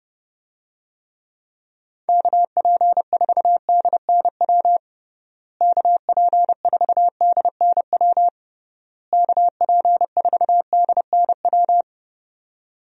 Morse code, heard as a keyed tone, about 32 wpm, 700 Hz. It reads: KP4DNW KP4DNW KP4DNW
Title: USA Call Signs - 6 Characters - Repeated 3x - 30wpm